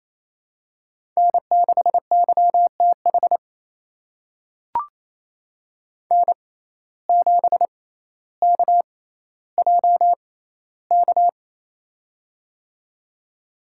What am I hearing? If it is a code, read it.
Morse code, heard as a keyed tone, about 28 wpm, spaced as for 14 wpm, 700 Hz. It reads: N6YTH N7KJK
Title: USA Call Signs - 5 Characters - 6x Character Spacing - 28wpm